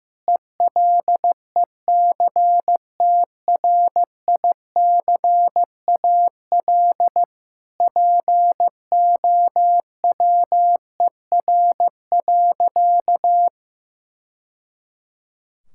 Morse code, heard as a keyed tone, about 15 wpm, 700 Hz. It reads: ELECTRICAL POWER.
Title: Sentences from Top 1000 Words - Review Part 6 - 15wpm